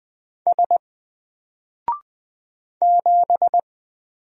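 Morse code, heard as a keyed tone, about 20 wpm, 700 Hz. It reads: S 7
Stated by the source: Single letter and number Morse drill at 20 wpm, from N4LCD.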